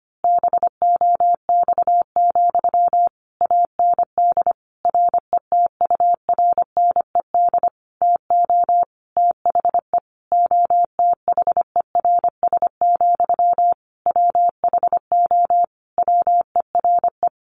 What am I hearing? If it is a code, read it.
Morse code, heard as a keyed tone, about 25 wpm, 700 Hz. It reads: BOX, AND RETURNED TO THE OTHERS, WHO WERE
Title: A Sound Of Thunder 25wpm